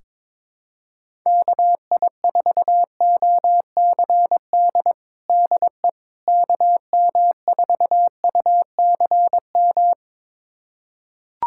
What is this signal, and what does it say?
Morse code, heard as a keyed tone, about 21 wpm, 700 Hz. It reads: KI4OCD DE KM4UCM
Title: USA Callsign DE Callsign - 6 Characters - 22wpm